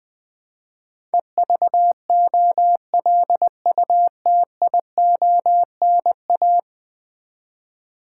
Morse code, heard as a keyed tone, about 20 wpm, 700 Hz. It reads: EVOLUTIONA
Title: Build A Word - 5 Letters At A Time - 12-Letter Words - 20wpm